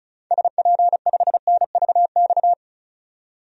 Morse code, heard as a keyed tone, about 36 wpm, 700 Hz. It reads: SP5NVX
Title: DXCC-100 Call Signs - 6 Characters - 35wpm